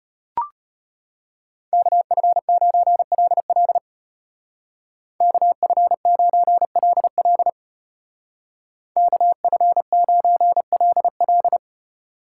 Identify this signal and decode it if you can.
Morse code, heard as a keyed tone, about 34 wpm, 700 Hz. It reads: KF9LL KF9LL KF9LL